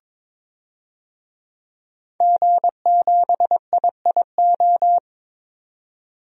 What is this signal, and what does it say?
Morse code, heard as a keyed tone, about 22 wpm, 700 Hz. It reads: G7IIO
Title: DXCC-100 Call Signs - 5 Characters - 22wpm